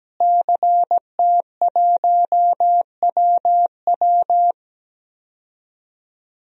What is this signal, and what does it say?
Morse code, heard as a keyed tone, about 17 wpm, 700 Hz. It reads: CT1WW
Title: DXCC-100 Call Signs - 5 Characters - Repeated 3x - 17wpm